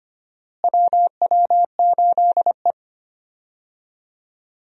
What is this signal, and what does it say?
Morse code, heard as a keyed tone, about 25 wpm, 700 Hz. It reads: WW8E
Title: USA Call Signs - 4 Characters - Repeated 2x - 25wpm